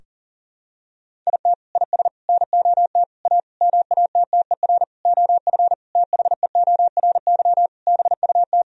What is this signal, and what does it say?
Morse code, heard as a keyed tone, about 40 wpm, 700 Hz. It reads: IT IS NOT A MATTER OF THEORY BUT
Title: Sentences from Top 1000 Words - Review Part 4 - 40wpm